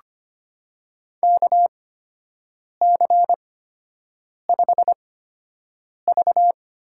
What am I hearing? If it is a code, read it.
Morse code, heard as a keyed tone, about 25 wpm, spaced as for 7 wpm, 700 Hz. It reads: KC5V
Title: USA Call Signs - 6 Characters - 8x Character Spacing - 25wpm